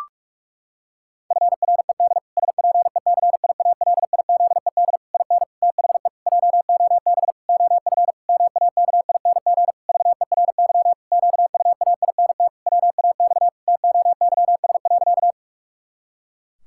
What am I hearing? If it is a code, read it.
Morse code, heard as a keyed tone, about 46 wpm, 700 Hz. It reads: FRED SPECIALIZED IN THE JOB OF MAKING VERY QUAINT WAX TOYS.
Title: Pangram Challenge v2 - 45wpm